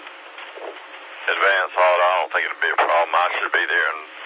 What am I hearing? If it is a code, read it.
Emergency Alert System (SAME) tones, heard as harsh buzzing bursts, with speech. It is heard from 1.2 to 3.9 s.